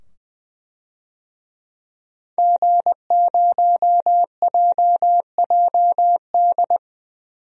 Morse code, heard as a keyed tone, about 20 wpm, 700 Hz. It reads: G0JJD
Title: DXCC-100 Call Signs - 5 Characters - 20wpm